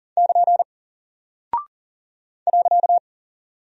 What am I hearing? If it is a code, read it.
Morse code, heard as a keyed tone, about 40 wpm, 700 Hz. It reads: <KN> .
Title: Single Character (Letter:Number:Punctuation:Prosign) Alt 40wpm